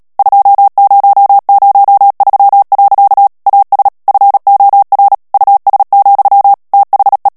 Morse code, heard as a keyed tone, about 37 wpm, 800 Hz. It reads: 2003. AS FOR US, THE
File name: morse mp3 / ANGLAIS texte clair 2 vitesse 35 mots minute